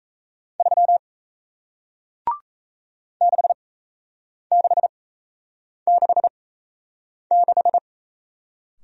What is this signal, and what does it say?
Morse code, heard as a keyed tone, about 38 wpm, spaced as for 8 wpm, 700 Hz. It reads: 3 6666